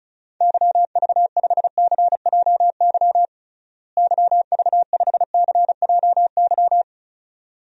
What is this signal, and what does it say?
Morse code, heard as a keyed tone, about 34 wpm, 700 Hz. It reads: YV5CJY YV5CJY
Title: DXCC-100 Call Signs - 6 Characters - Repeated 2x - 35wpm